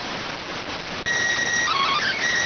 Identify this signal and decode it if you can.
SSTV (slow-scan television), PD 290.